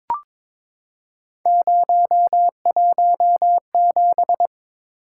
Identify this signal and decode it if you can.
Morse code, heard as a keyed tone, about 22 wpm, 700 Hz. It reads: 017
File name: practice / 22wpm / Three Letter-Number Combinations - 22wpm